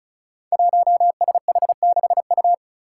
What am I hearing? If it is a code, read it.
Morse code, heard as a keyed tone, about 36 wpm, 700 Hz. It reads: 1SH6U
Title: Five Challenging Letter-Number Combinations 35wpm